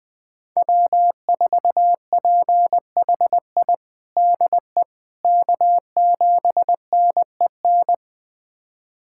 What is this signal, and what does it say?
Morse code, heard as a keyed tone, about 20 wpm, 700 Hz. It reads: W4PHI DE K7NEN
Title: USA Callsign DE Callsign - 5 Characters - 20wpm